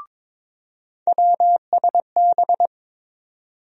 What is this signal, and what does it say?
Morse code, heard as a keyed tone, about 22 wpm, 700 Hz. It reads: WSB
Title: USA Call Sign Postfixes - 3 Letters 22wpm